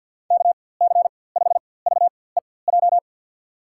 Morse code, heard as a keyed tone, about 52 wpm, spaced as for 23 wpm, 700 Hz. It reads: KC5VEJ
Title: USA Call Signs - 6 Characters - 4x Character Spacing - 50wpm